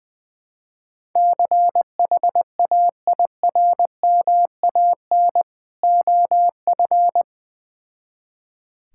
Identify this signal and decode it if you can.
Morse code, heard as a keyed tone, about 20 wpm, 700 Hz. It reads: CHAIRMAN OF
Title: Build A Phrase - 3-Word Phrases - 20wpm